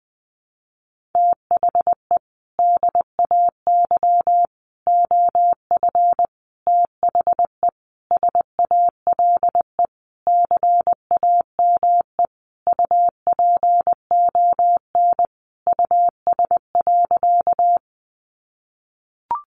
Morse code, heard as a keyed tone, about 20 wpm, 700 Hz. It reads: THE DAY OF THE SALE CAME UPON US.
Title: Sentences from Top 1400 Words 20wpm